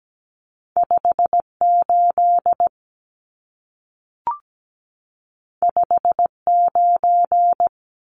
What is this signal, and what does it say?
Morse code, heard as a keyed tone, about 17 wpm, 700 Hz. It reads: 58 59
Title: Numbers - 2-Digit 17wpm